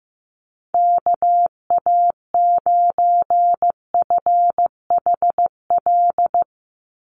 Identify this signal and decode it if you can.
Morse code, heard as a keyed tone, about 15 wpm, 700 Hz. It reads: KA9FHL
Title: USA Call Signs 15wpm